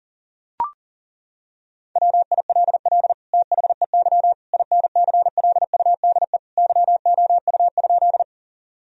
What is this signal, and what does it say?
Morse code, heard as a keyed tone, about 40 wpm, 700 Hz. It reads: WILL THEY INCLUDE YOU?
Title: Sentences from Top 1000 Words - Review Part 3 - 40wpm